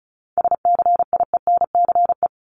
Morse code, heard as a keyed tone, about 34 wpm, 700 Hz. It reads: SCIENCE
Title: Top 500 Words 35wpm